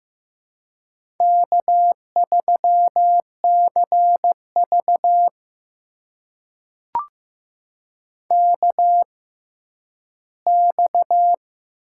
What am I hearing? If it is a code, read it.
Morse code, heard as a keyed tone, about 15 wpm, spaced as for 10 wpm, 700 Hz. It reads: K3CV KX
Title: USA Call Signs - 4 Characters - 6x Character Spacing - 15wpm